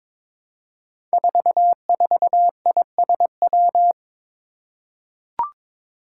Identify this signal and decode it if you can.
Morse code, heard as a keyed tone, about 22 wpm, 700 Hz. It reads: 44ISW
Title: Five Challenging Letter-Number Combinations 22wpm